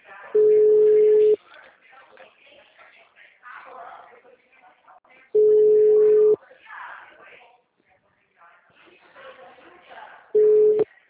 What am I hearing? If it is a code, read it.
Ringback tone (Europe, 425 Hz), with speech.
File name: out- -2008- - - .28582